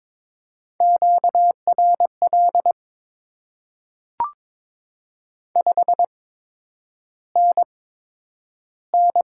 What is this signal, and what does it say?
Morse code, heard as a keyed tone, about 22 wpm, spaced as for 9 wpm, 700 Hz. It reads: QRL 5NN
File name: Top 20 QSO Elements - 8x Character Spacing - 22wpm